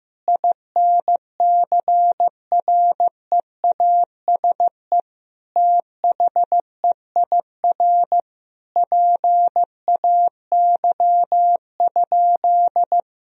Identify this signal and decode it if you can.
Morse code, heard as a keyed tone, about 15 wpm, 700 Hz. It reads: INCREASE THEIR PAY?
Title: Sentences from Top 1000 Words - Review Part 4 - 15wpm